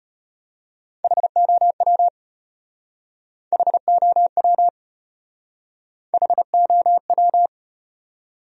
Morse code, heard as a keyed tone, about 32 wpm, 700 Hz. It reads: HOW HOW HOW